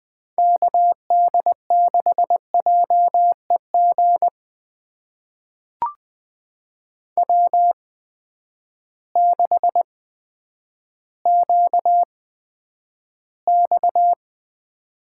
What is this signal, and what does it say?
Morse code, heard as a keyed tone, about 20 wpm, spaced as for 10 wpm, 700 Hz. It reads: KD6JEG W6QX
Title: USA Call Signs - 8x Character Spacing - 20wpm